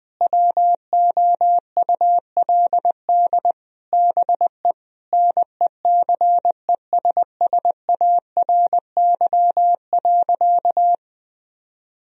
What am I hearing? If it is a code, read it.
Morse code, heard as a keyed tone, about 21 wpm, 700 Hz. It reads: WOULD BE NECESSARY.
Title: Sentences from Top 1000 Words - Review Part 3 - 20wpm